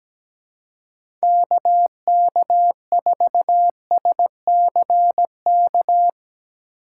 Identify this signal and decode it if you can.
Morse code, heard as a keyed tone, about 17 wpm, 700 Hz. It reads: KK4SCK
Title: USA Call Signs - 6 Characters - 4x Character Spacing - 17wpm